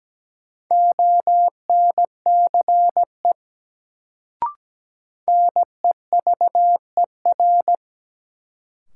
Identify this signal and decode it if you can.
Morse code, heard as a keyed tone, about 17 wpm, 700 Hz. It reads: ONCE NEVER
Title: Sentences from Top 300 Words - New Word Drill - 17wpm